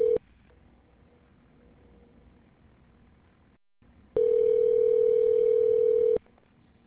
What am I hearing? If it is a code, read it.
Ringback tone (North America), 440 + 480 Hz.